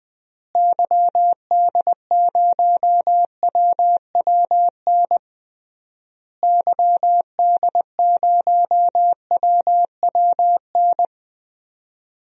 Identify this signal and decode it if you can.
Morse code, heard as a keyed tone, about 20 wpm, 700 Hz. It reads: YD0WWN YD0WWN
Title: DXCC-100 Call Signs - 6 Characters - Repeated 3x - 20wpm